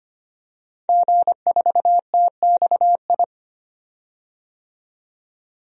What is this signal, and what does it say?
Morse code, heard as a keyed tone, about 25 wpm, 700 Hz. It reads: G4TXI
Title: DXCC-100 Call Signs - 5 Characters - 25wpm